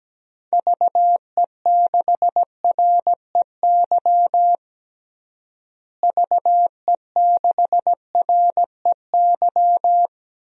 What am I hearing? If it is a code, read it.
Morse code, heard as a keyed tone, about 17 wpm, 700 Hz. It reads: VE6REY VE6REY
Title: DXCC-100 Call Signs - 6 Characters - Repeated 3x - 17wpm